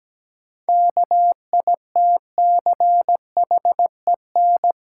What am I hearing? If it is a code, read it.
Morse code, heard as a keyed tone, about 17 wpm, 700 Hz. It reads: KITCHEN
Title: Sentences from Top 1400 Words - New Word Drill - 17wpm